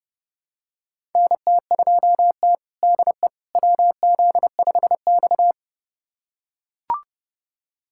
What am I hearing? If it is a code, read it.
Morse code, heard as a keyed tone, about 30 wpm, 700 Hz. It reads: NT2T DE WZ5X
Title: USA Callsign DE Callsign - 4 Characters - 30wpm